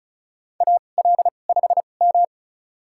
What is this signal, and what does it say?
Morse code, heard as a keyed tone, about 34 wpm, spaced as for 25 wpm, 700 Hz. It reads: AL5M
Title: USA Call Signs - 4 Characters - 2x Character Spacing - 35wpm